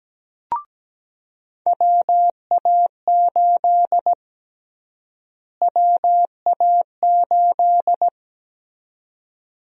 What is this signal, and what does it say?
Morse code, heard as a keyed tone, about 17 wpm, 700 Hz. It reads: WA8 WA8
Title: USA Call Sign Prefixes - Repeated 2x - 17wpm